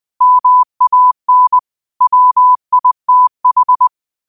Morse code, heard as a keyed tone, about 20 wpm, 1000 Hz. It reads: MAN WITH
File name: qotdfiles / 20wpm